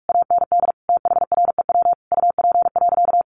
Morse code, heard as a keyed tone, about 44 wpm, 700 Hz. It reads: AND THREW UP.
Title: A Sound Of Thunder 45wpm